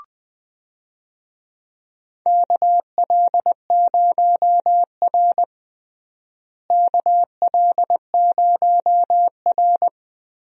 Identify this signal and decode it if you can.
Morse code, heard as a keyed tone, about 20 wpm, 700 Hz. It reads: KL0R KL0R
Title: USA Call Signs - 4 Characters - Repeated 2x - 20wpm